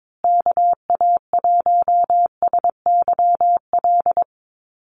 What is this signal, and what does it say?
Morse code, heard as a keyed tone, about 22 wpm, 700 Hz. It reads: KA1SYL
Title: USA Call Signs Repeated 3x 22wpm